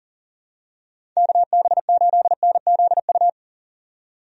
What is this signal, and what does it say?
Morse code, heard as a keyed tone, about 38 wpm, 700 Hz. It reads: KB8NZU
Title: USA Call Signs - 6 Characters - 40wpm